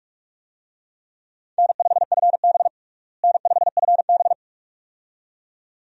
Morse code, heard as a keyed tone, about 46 wpm, 700 Hz. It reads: N5FB N5FB
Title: USA Call Signs - 4 Characters - Repeated 2x - 45wpm